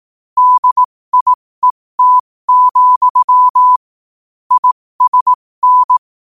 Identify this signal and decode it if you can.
Morse code, heard as a keyed tone, about 18 wpm, spaced as for 15 wpm, 1000 Hz. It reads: DIET, ISN
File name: qotdfiles / 15wpm